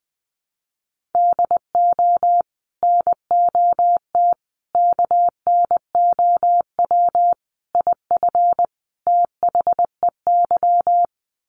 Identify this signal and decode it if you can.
Morse code, heard as a keyed tone, about 20 wpm, 700 Hz. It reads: DO NOT KNOW IF THEY
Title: Sets of 5 Words Encore 20wpm